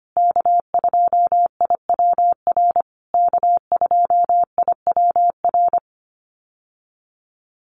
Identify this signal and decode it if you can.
Morse code, heard as a keyed tone, about 25 wpm, 700 Hz. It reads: K2IWR K2IWR
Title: USA Call Signs Repeated 3x 25wpm